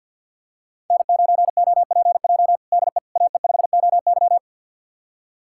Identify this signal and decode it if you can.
Morse code, heard as a keyed tone, about 49 wpm, 700 Hz. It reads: N9OPJ DE AE5OY
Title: USA Callsign DE Callsign - 5 Characters - 50wpm